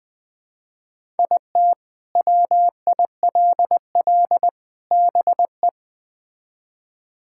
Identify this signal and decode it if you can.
Morse code, heard as a keyed tone, about 20 wpm, 700 Hz. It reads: IT WILL BE
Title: Build A Phrase - 3-Word Phrases - 20wpm